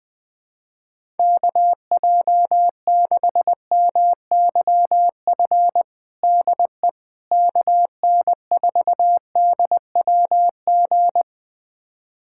Morse code, heard as a keyed tone, about 20 wpm, 700 Hz. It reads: KJ6MYF DE KN4DWG
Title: USA Callsign DE Callsign 20wpm